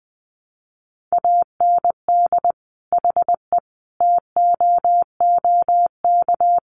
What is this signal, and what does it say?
Morse code, heard as a keyed tone, about 19 wpm, 700 Hz. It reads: AND HE TOOK
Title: Sentences from Top 1500 Words 20wpm